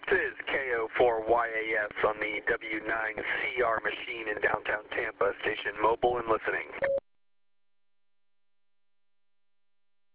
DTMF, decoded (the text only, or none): none